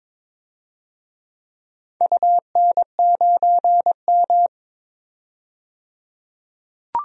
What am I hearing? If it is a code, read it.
Morse code, heard as a keyed tone, about 22 wpm, 700 Hz. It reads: UN9M
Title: DXCC-100 Call Signs - 4 Characters - 22wpm